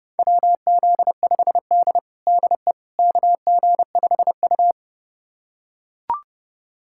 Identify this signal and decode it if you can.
Morse code, heard as a keyed tone, about 30 wpm, 700 Hz. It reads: WZ5D DE KG5U
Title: USA Callsign DE Callsign - 4 Characters - 30wpm